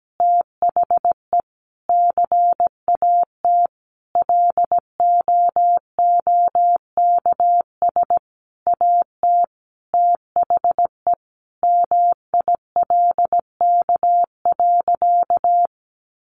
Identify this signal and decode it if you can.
Morse code, heard as a keyed tone, about 17 wpm, 700 Hz. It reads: THE CAT LOOKS AT THE MILK.
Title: Additional Easy Phrases 17wpm